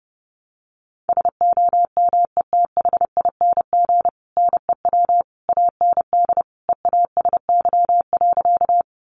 Morse code, heard as a keyed tone, about 30 wpm, 700 Hz. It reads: SOMETHING NEW AND EASY.
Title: Sentences from Top 700 Words 30wpm